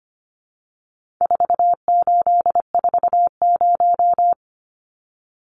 Morse code, heard as a keyed tone, about 25 wpm, 700 Hz. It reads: 4840
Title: Numbers - 4-Digit 25wpm